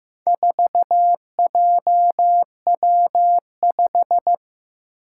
Morse code, heard as a keyed tone, about 15 wpm, 700 Hz. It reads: 4JW5
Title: Four Challenging Letter-Number Combinations 15wpm